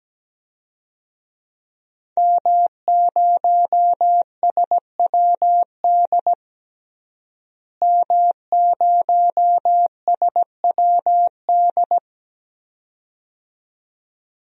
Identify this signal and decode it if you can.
Morse code, heard as a keyed tone, about 17 wpm, 700 Hz. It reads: M0SWD M0SWD
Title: DXCC-100 Call Signs - 5 Characters - Repeated 2x - 17wpm